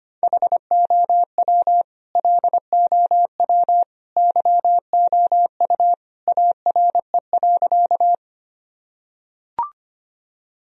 Morse code, heard as a keyed tone, about 25 wpm, 700 Hz. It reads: HOW LOW YOU ARE.
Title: Sentences from Top 1000 Words - Review Part 1 - 25wpm